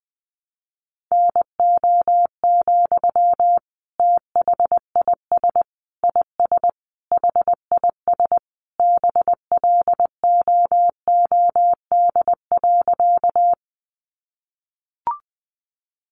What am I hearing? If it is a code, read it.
Morse code, heard as a keyed tone, about 20 wpm, 700 Hz. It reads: NO, THIS IS HIS BLOOD.
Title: Sentences from Top 1100 Words 20wpm